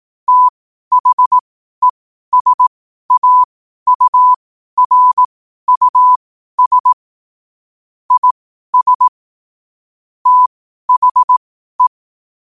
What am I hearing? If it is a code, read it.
Morse code, heard as a keyed tone, about 18 wpm, spaced as for 12 wpm, 1000 Hz. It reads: THESAURUS IS THE
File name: qotdfiles / 12wpm